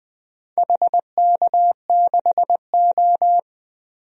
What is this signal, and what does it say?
Morse code, heard as a keyed tone, about 20 wpm, 700 Hz. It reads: HK6O
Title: DXCC-100 Call Signs - 4 Characters - 20wpm